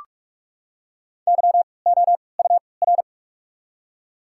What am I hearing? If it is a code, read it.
Morse code, heard as a keyed tone, about 46 wpm, spaced as for 25 wpm, 700 Hz. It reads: YOUR
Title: 3x Character Spacing - Top 500 Words - 45wpm